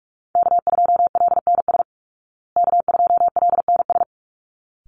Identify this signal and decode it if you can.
Morse code, heard as a keyed tone, about 44 wpm, 700 Hz. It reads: K2LNS K2LNS
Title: Call Signs Repeated 3x - 45wpm